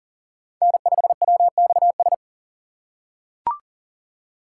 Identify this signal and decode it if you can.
Morse code, heard as a keyed tone, about 40 wpm, 700 Hz. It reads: N5WXS
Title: USA Call Signs - 5 Characters - 2x Character Spacing - 40wpm